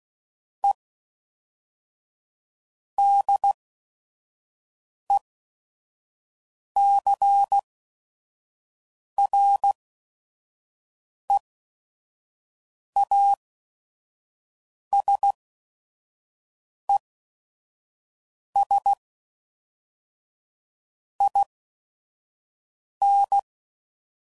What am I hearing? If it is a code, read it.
Morse code, heard as a keyed tone, about 16 wpm, spaced as for 5 wpm, 800 Hz. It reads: E DECREASES IN
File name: CW Code Groups @5WPM / Technical Subjects / XC